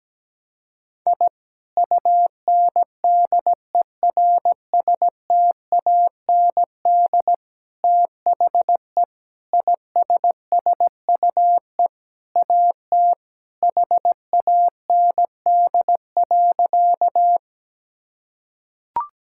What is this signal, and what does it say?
Morse code, heard as a keyed tone, about 17 wpm, 700 Hz. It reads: I UNDERSTAND THE ISSUE AT HAND.
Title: Sentences from Top 1000 Words - Review Part 6 - 17wpm